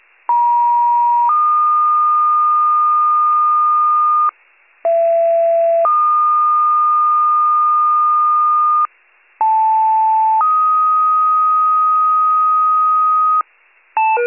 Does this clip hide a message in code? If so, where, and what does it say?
Yes: two-tone page from 0.3 to 13.4 s, heard as a one-second tone, then a three-second tone, about 950 then 1220 Hz, about 670 then 1220 Hz, about 880 then 1250 Hz.